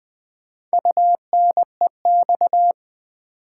Morse code, heard as a keyed tone, about 20 wpm, 700 Hz. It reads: UNEX
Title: Build A Word - 4 Letters At A Time - 10-Letter Words - 20wpm